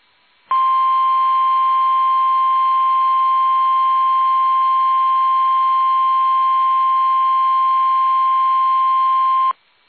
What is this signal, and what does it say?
Emergency Alert System (SAME) tones (the 1050 Hz attention tone). It is heard from 0.5 to 9.6 s.